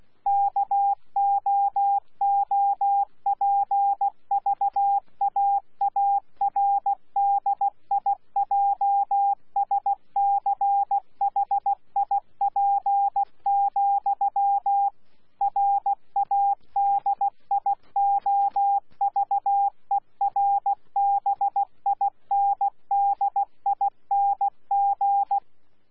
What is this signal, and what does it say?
Morse code, heard as a keyed tone, about 16 wpm, 800 Hz. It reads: KOOPVAARDIJSCHIP, RADIOVERBINDING
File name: Z PI4VRZ / PI4VRZ 14-jun-2025